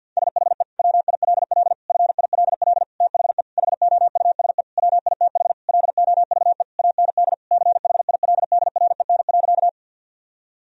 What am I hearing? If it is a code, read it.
Morse code, heard as a keyed tone, about 47 wpm, 700 Hz. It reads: SHE WILL FILL THE HOUSE WITH LOVE AND CHILDREN.
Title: Sentences from Top 1000 Words - Review Part 3 - 50wpm